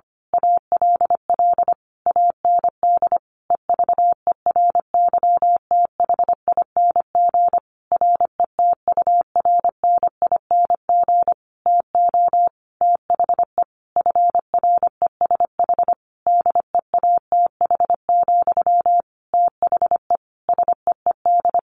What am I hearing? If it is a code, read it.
Morse code, heard as a keyed tone, about 25 wpm, 700 Hz. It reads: ALL AND EVERYTHING RETURNING TO THE FRESH DEATH, THE SEED
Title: A Sound Of Thunder 25wpm